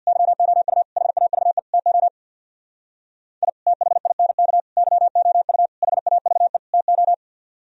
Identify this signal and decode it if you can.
Morse code, heard as a keyed tone, about 62 wpm, 700 Hz. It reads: YOU HAVE TO I THINK YOU HAVE TO
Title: Speed-Racing - Sets of 5 Words 50wpm